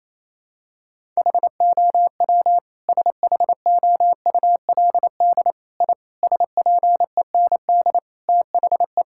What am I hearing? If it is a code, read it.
Morse code, heard as a keyed tone, about 27 wpm, 700 Hz. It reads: HOW SHOULD I SPEND THE
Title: Sentences from Top 1000 Words - Review Part 5 - 28wpm